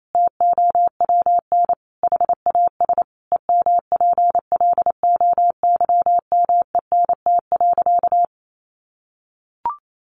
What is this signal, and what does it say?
Morse code, heard as a keyed tone, about 28 wpm, 700 Hz. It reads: TOWN HAS EMPLOYMENT.
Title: Sentences from Top 1500 Words 28wpm